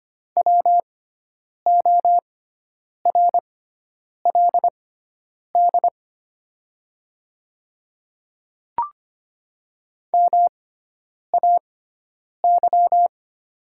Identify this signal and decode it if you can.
Morse code, heard as a keyed tone, about 25 wpm, spaced as for 9 wpm, 700 Hz. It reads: WORLD MAY
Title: 6x Character Spacing - Top 300 Words - 25wpm